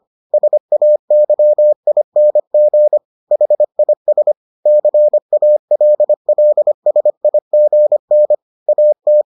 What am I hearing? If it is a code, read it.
Morse code, heard as a keyed tone, about 25 wpm, 600 Hz. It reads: SAYING HIS CALLSIGN AT